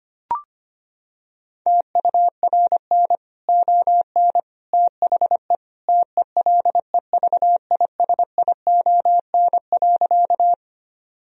Morse code, heard as a keyed tone, about 25 wpm, 700 Hz. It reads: TURN ON THE TELEVISION.
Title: Sentences from Top 1000 Words - Review Part 4 - 25wpm